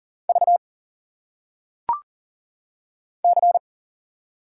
Morse code, heard as a keyed tone, about 39 wpm, 700 Hz. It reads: V C
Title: Single Letters - Phonetic Pronunciation 40wpm